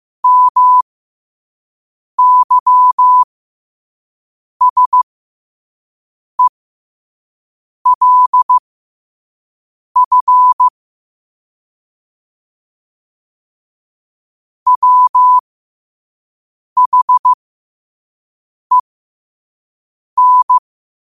Morse code, heard as a keyed tone, about 15 wpm, spaced as for 5 wpm, 1000 Hz. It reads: MYSELF WHEN